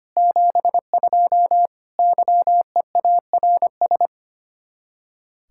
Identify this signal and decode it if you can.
Morse code, heard as a keyed tone, about 33 wpm, 700 Hz. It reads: 72 YEARS